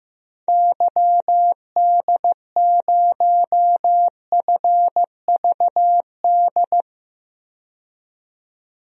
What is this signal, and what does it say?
Morse code, heard as a keyed tone, about 15 wpm, 700 Hz. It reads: YD0FVD
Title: DXCC-100 Call Signs - 6 Characters - 15wpm